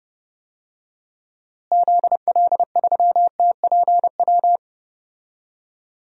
Morse code, heard as a keyed tone, about 30 wpm, 700 Hz. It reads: ZL3TPW
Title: DXCC-100 Call Signs - 6 Characters - Repeated 3x - 30wpm